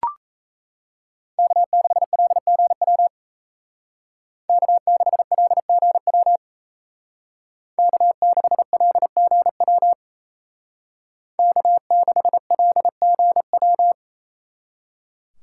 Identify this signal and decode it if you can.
Morse code, heard as a keyed tone, about 37 wpm, 700 Hz. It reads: K6LGW K6LGW K6LGW K6LGW